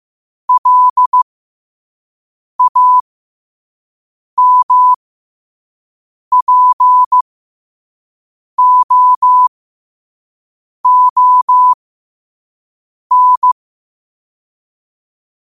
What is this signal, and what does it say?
Morse code, heard as a keyed tone, about 15 wpm, spaced as for 5 wpm, 1000 Hz. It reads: LAMPOON
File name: qotdfiles / 5wpm